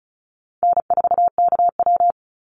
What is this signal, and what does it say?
Morse code, heard as a keyed tone, about 34 wpm, 700 Hz. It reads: N4KW
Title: Easy Call Signs Repeated 2x - 35wpm